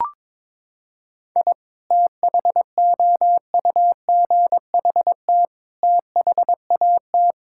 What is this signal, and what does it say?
Morse code, heard as a keyed tone, about 22 wpm, 700 Hz. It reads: I THOUGHT THAT
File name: practice / 22wpm / Sentences with 3-Word Phrases - 22wpm